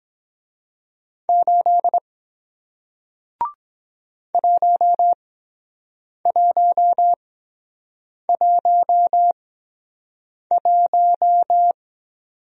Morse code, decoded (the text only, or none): 8 1111